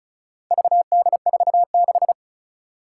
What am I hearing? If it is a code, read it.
Morse code, heard as a keyed tone, about 36 wpm, 700 Hz. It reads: VD46
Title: Four Challenging Letter-Number Combinations 35wpm